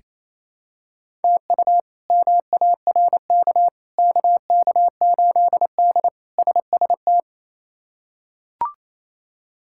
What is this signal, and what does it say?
Morse code, heard as a keyed tone, about 28 wpm, 700 Hz. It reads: TU MARK KK8D SST
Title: K1USN SST QSO Practice - 28wpm